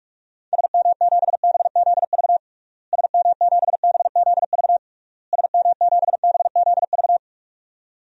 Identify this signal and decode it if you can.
Morse code, heard as a keyed tone, about 46 wpm, 700 Hz. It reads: SM7BZV SM7BZV SM7BZV
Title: DXCC-100 Call Signs - 6 Characters - Repeated 3x - 45wpm